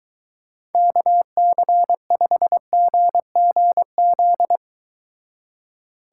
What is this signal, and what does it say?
Morse code, heard as a keyed tone, about 23 wpm, 700 Hz. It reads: KC5GGZ